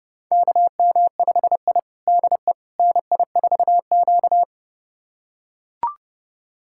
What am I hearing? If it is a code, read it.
Morse code, heard as a keyed tone, about 30 wpm, 700 Hz. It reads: KM5I DE NI4Q
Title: USA Callsign DE Callsign - 4 Characters - 30wpm